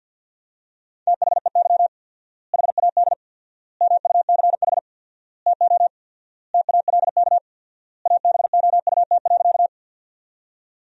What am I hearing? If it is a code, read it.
Morse code, heard as a keyed tone, about 51 wpm, 700 Hz. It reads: THEY HAD MUCH TO TALK ABOUT.
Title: 4x Word-Spacing - Sentences from Top 400 Words - 50wpm